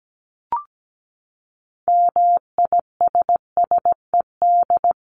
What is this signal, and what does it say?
Morse code, heard as a keyed tone, about 17 wpm, 700 Hz. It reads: MISSED
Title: Common QSO Phrases 17wpm